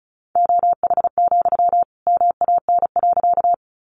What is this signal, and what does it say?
Morse code, heard as a keyed tone, about 34 wpm, 700 Hz. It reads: OH, MAN.
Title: Sentences from Top 500 Words 35wpm